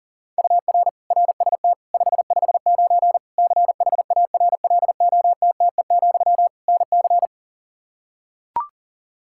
Morse code, heard as a keyed tone, about 40 wpm, 700 Hz. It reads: UR RST 559 CHARLOTTE, NC
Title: Practice QSO - Exchange 1 - Beginning - 40wpm